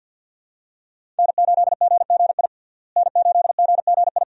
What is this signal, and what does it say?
Morse code, heard as a keyed tone, about 51 wpm, 700 Hz. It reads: N8GGI N8GGI
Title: USA Call Signs - 5 Characters - Repeated 2x - 50wpm